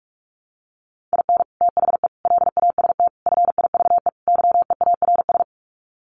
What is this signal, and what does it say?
Morse code, heard as a keyed tone, about 44 wpm, 700 Hz. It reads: IN THE LAST FIVE YEARS
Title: Sets of 5 Words Encore 45wpm